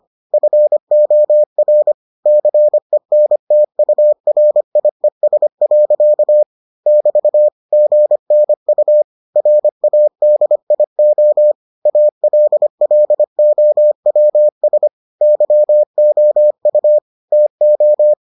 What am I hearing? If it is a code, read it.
Morse code, heard as a keyed tone, about 25 wpm, 600 Hz. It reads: FOR CENTURIES. <BT> GNU RADIO ALLOWS YOU TO